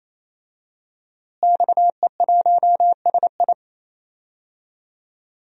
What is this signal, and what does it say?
Morse code, heard as a keyed tone, about 28 wpm, 700 Hz. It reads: XE1SI
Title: DXCC-100 Call Signs - 5 Characters - Repeated 3x - 28wpm